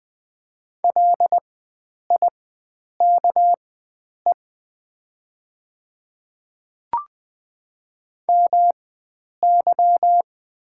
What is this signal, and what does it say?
Morse code, heard as a keyed tone, about 20 wpm, spaced as for 9 wpm, 690 Hz. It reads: LIKE MY
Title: 4x Character Spacing - Top 100 Words - 20wpm